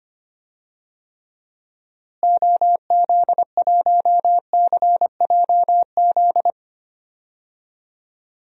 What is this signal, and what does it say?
Morse code, heard as a keyed tone, about 25 wpm, 700 Hz. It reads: OZ1CJZ
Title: DXCC-100 Call Signs - 6 Characters - Repeated 3x - 25wpm